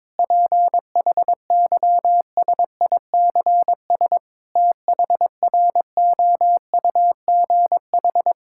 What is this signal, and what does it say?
Morse code, heard as a keyed tone, about 22 wpm, 700 Hz. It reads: PHYSICS THROUGH
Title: Sentences from Top 1000 Words - Review Part 5 - 22wpm